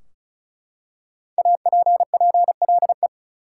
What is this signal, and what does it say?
Morse code, heard as a keyed tone, about 34 wpm, 700 Hz. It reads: APPLE
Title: Five Letter Words - 35wpm